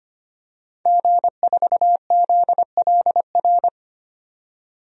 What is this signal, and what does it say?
Morse code, heard as a keyed tone, about 25 wpm, 700 Hz. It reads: G4ZLR
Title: DXCC-100 Call Signs - 5 Characters - 25wpm